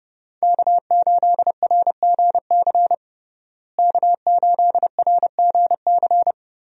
Morse code, heard as a keyed tone, about 32 wpm, 700 Hz. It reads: K8RGC K8RGC
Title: USA Call Signs - 5 Characters - Repeated 2x - 30wpm